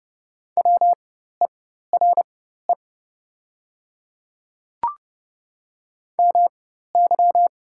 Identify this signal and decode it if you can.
Morse code, heard as a keyed tone, about 30 wpm, spaced as for 14 wpm, 700 Hz. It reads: WERE MY
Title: 4x Character Spacing - Top 100 Words - 30wpm